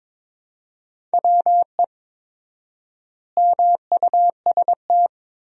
Morse code, heard as a keyed tone, about 22 wpm, 700 Hz. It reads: WE MUST
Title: 4x Word-Spacing - Sentences from Top 400 Words - 22wpm